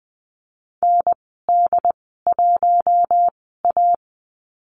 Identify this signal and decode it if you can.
Morse code, heard as a keyed tone, about 20 wpm, spaced as for 14 wpm, 700 Hz. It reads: ND1A
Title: 069 Lesson 12g - Letter d - Call Signs 20wpm